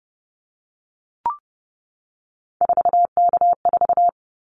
Morse code, heard as a keyed tone, about 30 wpm, 700 Hz. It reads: 4K4
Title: Callsign Prefixes for Top 100 DXCC - 30wpm